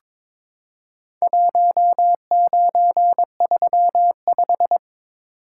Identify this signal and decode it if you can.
Morse code, heard as a keyed tone, about 22 wpm, 700 Hz. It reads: 1935
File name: practice / 22wpm / Four Letter-Number Combinations - 22wpm